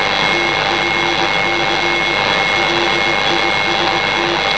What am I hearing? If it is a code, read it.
Morse code, heard as a keyed tone, about 20 wpm, 340 Hz, with noise. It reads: TFC LIST
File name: HZH Low Frequency